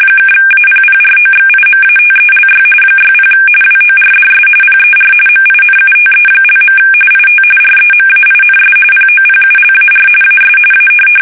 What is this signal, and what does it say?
RTTY, about 45.45 baud. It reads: (APPROXIMATELY 60 WORDS PER MINUTE). IT REMAINS POPULAR AS A
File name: fsk850 / teletype-RTTY